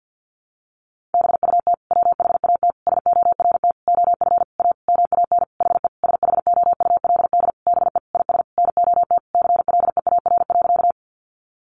Morse code, heard as a keyed tone, about 50 wpm, 700 Hz. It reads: BUT WHAT SORT OF A MAN HE SHOULD BE IS NOT CLEAR.
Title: Sentences from Top 1300 Words 50wpm